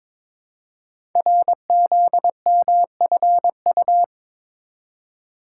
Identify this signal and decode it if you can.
Morse code, heard as a keyed tone, about 22 wpm, 700 Hz. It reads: RZMFU
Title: Code Groups - 5 Letter:Numbers - 22wpm